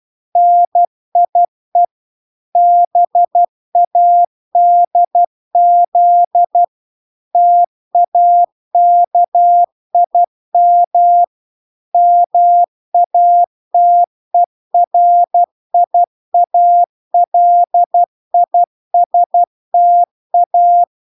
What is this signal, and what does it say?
Morse code, heard as a keyed tone, about 12 wpm, 700 Hz. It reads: NIE BADZ TAKIM MATERIALISTA